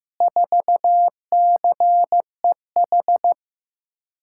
Morse code, heard as a keyed tone, about 15 wpm, 700 Hz. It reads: 4CEH
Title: USA Call Signs - 8x Character Spacing - 15wpm